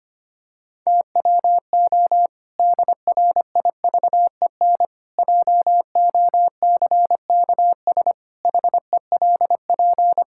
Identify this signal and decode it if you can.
Morse code, heard as a keyed tone, about 25 wpm, 700 Hz. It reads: TWO DRIVEN JOCKS HELP
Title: Pangram Challenge v2 - 25wpm